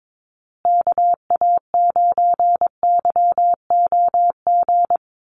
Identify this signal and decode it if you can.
Morse code, heard as a keyed tone, about 22 wpm, 700 Hz. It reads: KA9YOG
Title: USA Call Signs Repeated 2x 22wpm